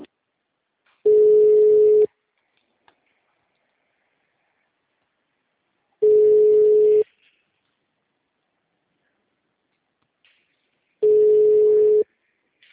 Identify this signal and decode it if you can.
Ringback tone (Europe, 425 Hz).